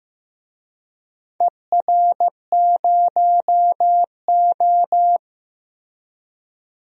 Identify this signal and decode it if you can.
Morse code, heard as a keyed tone, about 15 wpm, 700 Hz. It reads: ER0O
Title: DXCC-100 Call Signs - 4 Characters - 15wpm